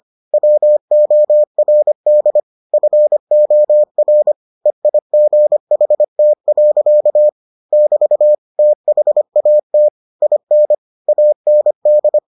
Morse code, heard as a keyed tone, about 25 wpm, 600 Hz. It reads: WORD FOR EIGHT. <BT> THAT IN AND